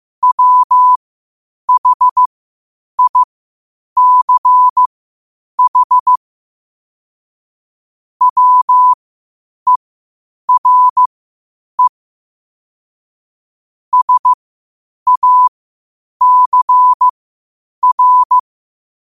Morse code, heard as a keyed tone, about 15 wpm, spaced as for 8 wpm, 1000 Hz. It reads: WHICH WERE SACR
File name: qotdfiles / 8wpm